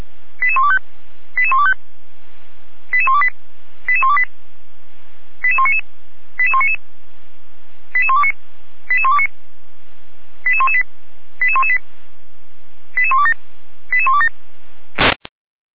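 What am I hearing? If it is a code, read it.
Five-tone selective call (ZVEI), heard as quick runs of five tones, tones of about 70 ms. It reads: 88125 88125 88128 88128 88199 88199 88129 88129 88118 88118 88127 88127
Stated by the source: SDR-received ZVEI five-tone call, code 88235.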